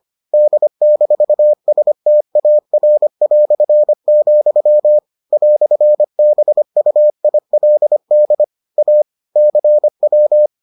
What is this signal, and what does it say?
Morse code, heard as a keyed tone, about 25 wpm, 600 Hz. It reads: D-STAR", "BUILD A CW